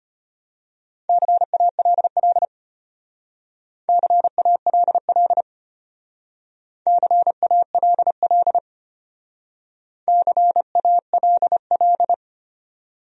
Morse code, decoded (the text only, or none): CALL CALL CALL CALL